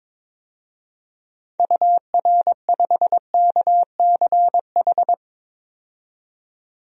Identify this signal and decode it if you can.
Morse code, heard as a keyed tone, about 22 wpm, 700 Hz. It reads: UR5KCH
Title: DXCC-100 Call Signs - 6 Characters - 22wpm